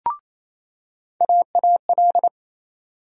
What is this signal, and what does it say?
Morse code, heard as a keyed tone, about 28 wpm, 700 Hz. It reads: AAL
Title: USA Call Sign Postfixes - 3 Letters 28wpm